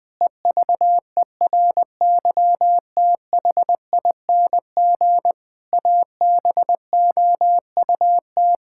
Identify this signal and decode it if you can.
Morse code, heard as a keyed tone, about 20 wpm, 700 Hz. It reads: EVERYTHING ABOUT
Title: Pangram Challenge v2 - 20wpm